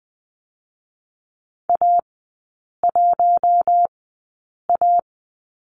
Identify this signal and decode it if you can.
Morse code, heard as a keyed tone, about 20 wpm, spaced as for 8 wpm, 700 Hz. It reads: A1A
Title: 041 Lesson 8e - Number 1 - Three Characters 20wpm